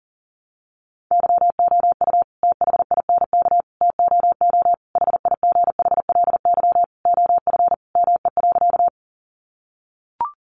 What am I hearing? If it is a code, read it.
Morse code, heard as a keyed tone, about 40 wpm, 700 Hz. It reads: YOU THINK TOO HIGHLY OF ME.
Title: Sentences from Top 1100 Words 40wpm